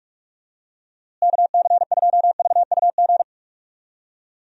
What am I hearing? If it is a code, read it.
Morse code, heard as a keyed tone, about 46 wpm, 700 Hz. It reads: KC2VUG
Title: USA Call Signs - 6 Characters - Repeated 3x - 45wpm